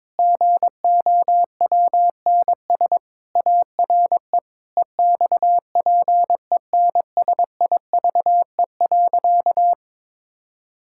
Morse code, heard as a keyed tone, about 22 wpm, 700 Hz. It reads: GOWNS ARE EXPENSIVE.
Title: Pangram Challenge v2 - 22wpm